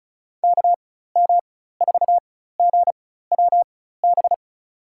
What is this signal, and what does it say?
Morse code, heard as a keyed tone, about 34 wpm, spaced as for 16 wpm, 700 Hz. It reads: KM4GWB
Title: USA Call Signs - 6 Characters - 4x Character Spacing - 35wpm